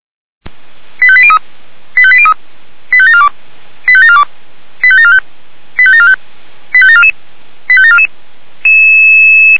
Five-tone selective call (ZVEI), heard as quick runs of five tones, tones of about 70 ms. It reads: 85803 85803 85742 85742 85745 85745 85744 85744